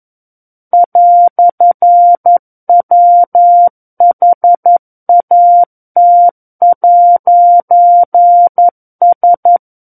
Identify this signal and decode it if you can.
Morse code, heard as a keyed tone, about 11 wpm, 700 Hz. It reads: "WHAT'S
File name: Book One / 02-Chapter 2 - 11 WPM